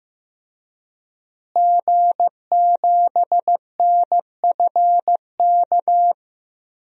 Morse code, heard as a keyed tone, about 15 wpm, 700 Hz. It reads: G7NFK
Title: DXCC-100 Call Signs - 5 Characters - Repeated 3x - 15wpm